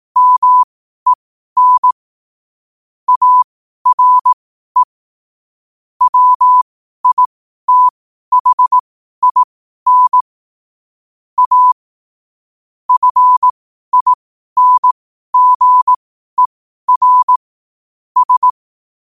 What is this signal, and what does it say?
Morse code, heard as a keyed tone, about 18 wpm, spaced as for 12 wpm, 1000 Hz. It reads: MEN ARE WITHIN A FINGER S
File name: qotdfiles / 12wpm